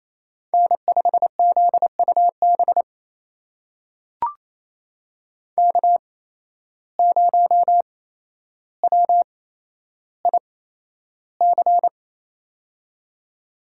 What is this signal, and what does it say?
Morse code, heard as a keyed tone, about 28 wpm, spaced as for 12 wpm, 700 Hz. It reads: N5ZUB K0WIC
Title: USA Call Signs - 5 Characters - 8x Character Spacing - 28wpm